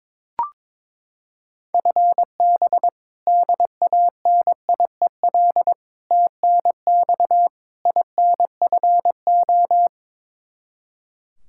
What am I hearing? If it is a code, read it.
Morse code, heard as a keyed tone, about 22 wpm, 700 Hz. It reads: FB DANIEL TNX INFO
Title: Practice QSO - Exchange 3 - More Info - 22wpm